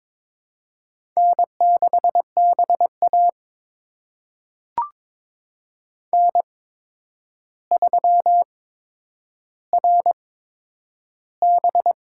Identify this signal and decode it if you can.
Morse code, heard as a keyed tone, about 22 wpm, spaced as for 9 wpm, 700 Hz. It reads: N6BA N3RB
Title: USA Call Signs - 4 Characters - 8x Character Spacing - 22wpm